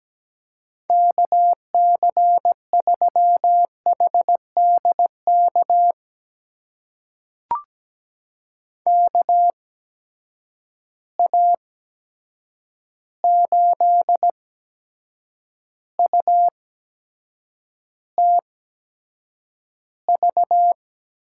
Morse code, heard as a keyed tone, about 17 wpm, spaced as for 7 wpm, 700 Hz. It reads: KC3HDK KA8UTV
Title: USA Call Signs - 6 Characters - 8x Character Spacing - 17wpm